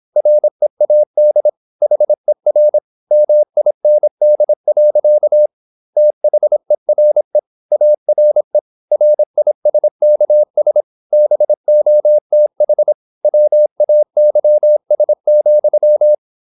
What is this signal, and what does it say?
Morse code, heard as a keyed tone, about 26 wpm, 600 Hz. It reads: READ HER MIND. THERE ARE RISKS BOTH WAYS,
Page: Never 26wpm | CW med Gnister